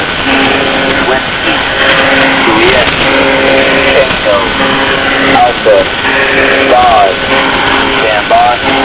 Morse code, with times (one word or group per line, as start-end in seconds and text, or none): none